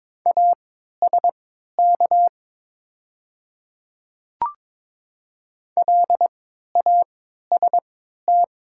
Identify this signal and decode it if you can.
Morse code, heard as a keyed tone, about 22 wpm, spaced as for 12 wpm, 700 Hz. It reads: ASK LAST
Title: 3x Character Spacing - Top 500 Words - 22wpm